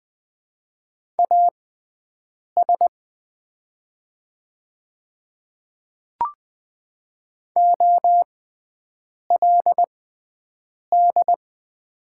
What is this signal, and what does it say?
Morse code, heard as a keyed tone, about 20 wpm, spaced as for 7 wpm, 700 Hz. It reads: AS OLD
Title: 6x Character Spacing - Top 300 Words - 20wpm